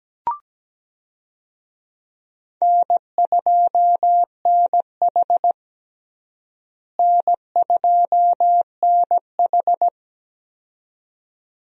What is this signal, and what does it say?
Morse code, heard as a keyed tone, about 17 wpm, 700 Hz. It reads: N2NH N2NH
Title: USA Call Signs - 4 Characters - Repeated 2x - 17wpm